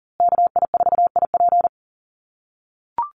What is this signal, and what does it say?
Morse code, heard as a keyed tone, about 40 wpm, 700 Hz. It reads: KI4IP
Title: USA Call Signs Repeated 3x 40wpm